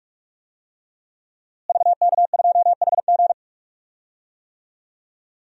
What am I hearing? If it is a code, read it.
Morse code, heard as a keyed tone, about 45 wpm, 700 Hz. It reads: VK2HG
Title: DXCC-100 Call Signs - 5 Characters - Repeated 3x - 45wpm